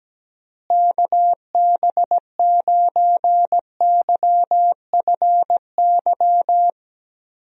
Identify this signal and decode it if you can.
Morse code, heard as a keyed tone, about 17 wpm, 700 Hz. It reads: KB9YFY
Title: USA Call Signs - 6 Characters - Repeated 3x - 17wpm